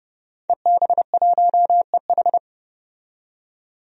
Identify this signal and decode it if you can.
Morse code, heard as a keyed tone, about 30 wpm, 700 Hz. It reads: EB1EH
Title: DXCC-100 Call Signs - 5 Characters - 30wpm